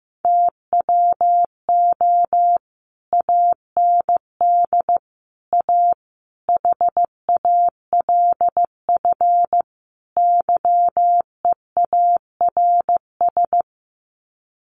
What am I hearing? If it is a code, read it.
Morse code, heard as a keyed tone, about 15 wpm, 700 Hz. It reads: TWO AND A HALF YEARS
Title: Sets of 5 Words 15wpm